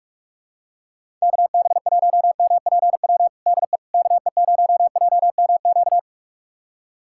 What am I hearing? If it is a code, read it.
Morse code, heard as a keyed tone, about 46 wpm, 700 Hz. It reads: KB1MPW DE KE0JMQ
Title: USA Callsign DE Callsign - 6 Characters - 45wpm